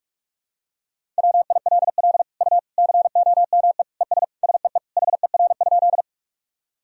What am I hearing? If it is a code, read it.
Morse code, heard as a keyed tone, about 46 wpm, 700 Hz. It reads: WILL U COME ES SEE HER?
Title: Sentences from Top 100 Words and Abbreviations 45wpm